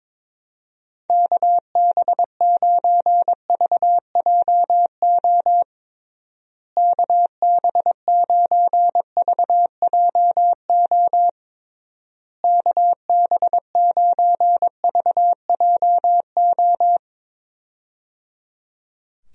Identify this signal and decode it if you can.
Morse code, heard as a keyed tone, about 22 wpm, 700 Hz. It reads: KB9VJO KB9VJO KB9VJO
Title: USA Call Signs - 6 Characters - Repeated 3x - 22wpm